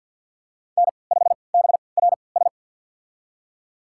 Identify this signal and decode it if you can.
Morse code, heard as a keyed tone, about 45 wpm, spaced as for 27 wpm, 700 Hz. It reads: N5BRS
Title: USA Call Signs - 3x Character Spacing - 50wpm